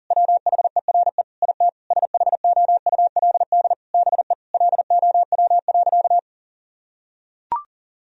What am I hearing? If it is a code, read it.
Morse code, heard as a keyed tone, about 40 wpm, 700 Hz. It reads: WHERE IT SHOULD BE LOW.
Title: Sentences from Top 1000 Words - Review Part 1 - 40wpm